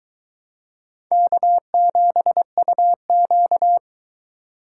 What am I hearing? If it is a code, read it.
Morse code, heard as a keyed tone, about 23 wpm, 700 Hz. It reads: K7UQ